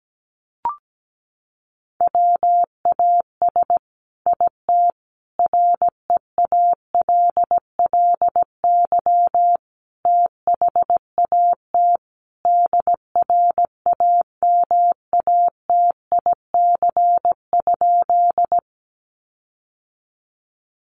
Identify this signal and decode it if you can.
Morse code, heard as a keyed tone, about 17 wpm, 700 Hz. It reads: WAS IT REALLY THAT DRAMATIC?
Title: Sentences from Top 1300 Words 17wpm